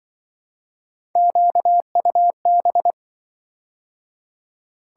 Morse code, heard as a keyed tone, about 25 wpm, 700 Hz. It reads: QUB
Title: Speed-Racing - Three Letter-Number Combinations - 20wpm